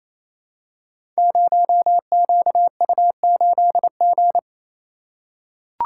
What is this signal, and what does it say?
Morse code, heard as a keyed tone, about 29 wpm, 700 Hz. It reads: 0QU8G
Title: Code Groups - 5 Letter:Numbers - 28wpm